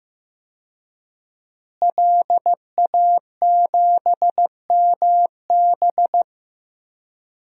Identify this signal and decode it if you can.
Morse code, heard as a keyed tone, about 15 wpm, 700 Hz. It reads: LA7MB
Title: DXCC-100 Call Signs - 5 Characters - Repeated 2x - 15wpm